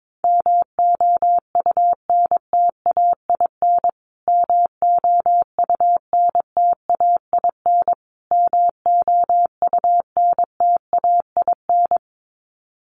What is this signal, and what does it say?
Morse code, heard as a keyed tone, about 22 wpm, 700 Hz. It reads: MOUNTAIN MOUNTAIN MOUNTAIN
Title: Top 500 Words Repeated 22wpm